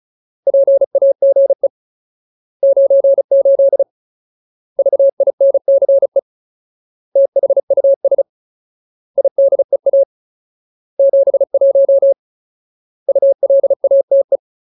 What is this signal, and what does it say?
Morse code, heard as a keyed tone, about 34 wpm, 550 Hz. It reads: PAGE 98 VINCE THUS IDEA 71 ULATE
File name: Ultimate Word List / 35WPM Ultimate